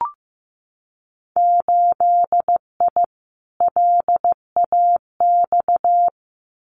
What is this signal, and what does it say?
Morse code, heard as a keyed tone, about 15 wpm, 700 Hz. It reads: 8I LAX
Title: Winter Field Day Exchanges Encore 15wpm